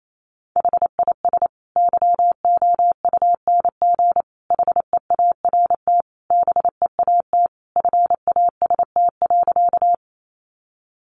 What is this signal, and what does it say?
Morse code, heard as a keyed tone, about 28 wpm, 700 Hz. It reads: HIS YOUNG HEART BEAT FAST.
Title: Sentences from Top 1300 Words 28wpm